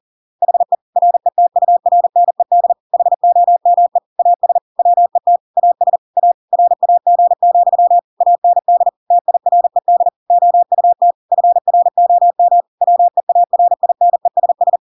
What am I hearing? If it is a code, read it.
Morse code, heard as a keyed tone, about 40 wpm, 700 Hz. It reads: HE RETURNED HOME AS WET AS A RAG, AND TIRED OUT FROM WEARINESS
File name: Pinocchio / Pinocchio ch6 40 40 ik1hge